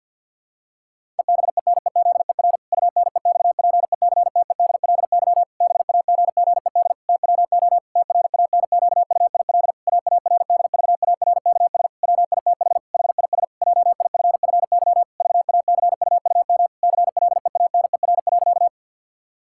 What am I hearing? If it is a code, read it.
Morse code, heard as a keyed tone, about 50 wpm, 700 Hz. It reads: EBENEZER UNEXPECTEDLY BAGGED TWO TRANQUIL AARDVARKS WITH HIS JIFFY VACUUM CLEANER.
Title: Pangram Challenge v2 - 50wpm